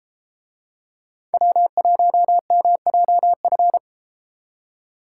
Morse code, heard as a keyed tone, about 32 wpm, 700 Hz. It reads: W1MJF